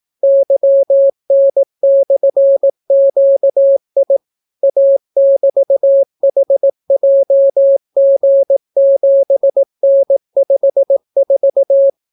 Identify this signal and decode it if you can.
Morse code, heard as a keyed tone, about 18 wpm, 550 Hz. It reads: YN/QI A<BT>HJG7N54